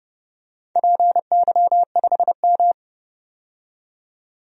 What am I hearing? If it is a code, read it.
Morse code, heard as a keyed tone, about 32 wpm, 700 Hz. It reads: PY5M
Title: DXCC-100 Call Signs - 4 Characters - 30wpm